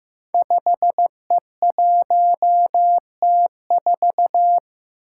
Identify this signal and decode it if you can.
Morse code, heard as a keyed tone, about 15 wpm, 700 Hz. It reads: 5E1T4
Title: Five Challenging Letter-Number Combinations 15wpm